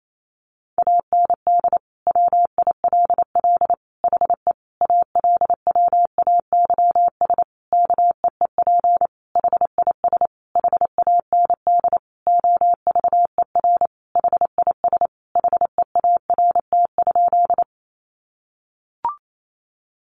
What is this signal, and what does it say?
Morse code, heard as a keyed tone, about 28 wpm, 700 Hz. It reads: AND WILL HE ALWAYS KEEP HIS HAND OVER HIS HEART?
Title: Sentences from Top 1200 Words 28wpm